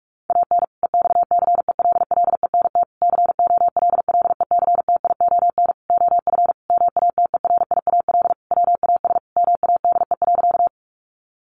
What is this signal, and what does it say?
Morse code, heard as a keyed tone, about 44 wpm, 700 Hz. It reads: AN EXCELLENT COLLECTION OF MATERIAL WAS MADE.
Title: Sentences from Top 1200 Words 45wpm